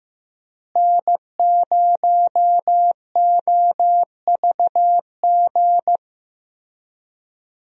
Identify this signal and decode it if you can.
Morse code, heard as a keyed tone, about 15 wpm, 700 Hz. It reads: N0OVG
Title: USA Call Signs - 5 Characters - Repeated 3x - 15wpm